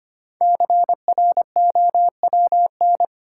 Morse code, heard as a keyed tone, about 25 wpm, 700 Hz. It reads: CROWN
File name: 5 Letter Words - First 3 Letters Given - 25wpm